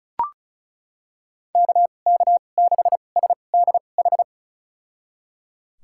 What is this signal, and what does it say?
Morse code, heard as a keyed tone, about 36 wpm, spaced as for 25 wpm, 700 Hz. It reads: KK6SDH
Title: USA Call Signs - 6 Characters - 2x Character Spacing - 35wpm